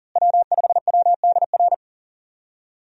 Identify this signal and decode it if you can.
Morse code, heard as a keyed tone, about 40 wpm, 700 Hz. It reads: W5WDR
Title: USA Call Signs - 5 Characters - Repeated 2x - 40wpm